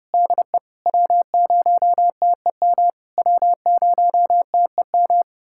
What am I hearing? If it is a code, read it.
Morse code, heard as a keyed tone, about 32 wpm, 700 Hz. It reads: DE W0TEM W0TEM
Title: SOTA QSO Practice 30wpm